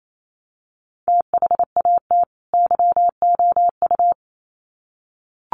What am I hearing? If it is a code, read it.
Morse code, heard as a keyed tone, about 28 wpm, 700 Hz. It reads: THAT YOU
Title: Sets of 2 Words 28wpm